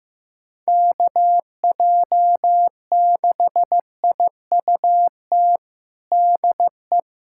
Morse code, heard as a keyed tone, about 16 wpm, 700 Hz. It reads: KJ6IUT DE
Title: USA Callsign DE Callsign 15wpm